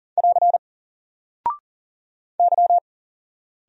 Morse code, heard as a keyed tone, about 41 wpm, 700 Hz. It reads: <AR> Y
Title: Single Character (Letter:Number:Punctuation:Prosign) 40wpm